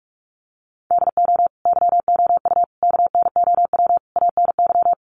Morse code, heard as a keyed tone, about 44 wpm, 700 Hz. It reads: DO YOU KNOW ANY
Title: Sentences from Top 1200 Words 45wpm